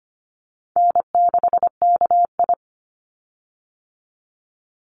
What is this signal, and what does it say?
Morse code, heard as a keyed tone, about 25 wpm, 700 Hz. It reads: N6KI
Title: Call Signs Repeated 3x - 25wpm